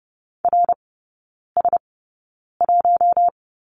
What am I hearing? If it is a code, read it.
Morse code, heard as a keyed tone, about 30 wpm, spaced as for 9 wpm, 690 Hz. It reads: RS1
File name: Lesson 17e - Letter u - Three Characters 30wpm